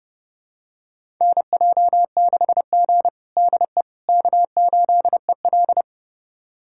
Morse code, heard as a keyed tone, about 32 wpm, 700 Hz. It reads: NJ6G DE K8EL
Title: USA Callsign DE Callsign - 4 Characters - 30wpm